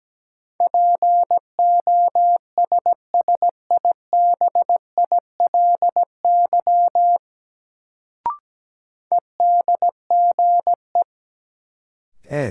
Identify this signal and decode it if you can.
Morse code, heard as a keyed tone, about 17 wpm, 700 Hz. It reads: POSSIBILY EDGE
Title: Sentences from Top 1300 Words - New Word Drill - 17wpm